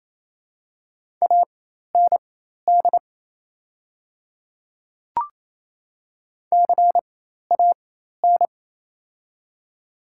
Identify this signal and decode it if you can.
Morse code, heard as a keyed tone, about 28 wpm, spaced as for 13 wpm, 700 Hz. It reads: AND CAN
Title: 4x Character Spacing - Top 300 Words - 28wpm